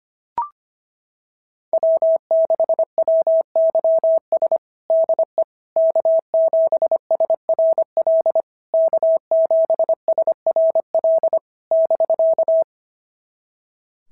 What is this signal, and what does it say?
Morse code, heard as a keyed tone, about 25 wpm, 650 Hz. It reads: W6WYS DE K7SRL K7SRL <BK>
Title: SOTA QSO Practice 25wpm